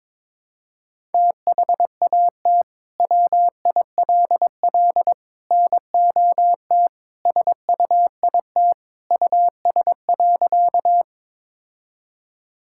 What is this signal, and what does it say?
Morse code, heard as a keyed tone, about 22 wpm, 700 Hz. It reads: THAT WILL NOT SUIT US.
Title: Sentences from Top 1600 Words 22wpm